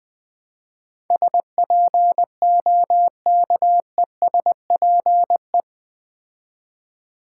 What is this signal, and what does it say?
Morse code, heard as a keyed tone, about 20 wpm, 700 Hz. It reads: SPOKESPE
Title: Build A Word - 4 Letters At A Time - 12-Letter Words - 20wpm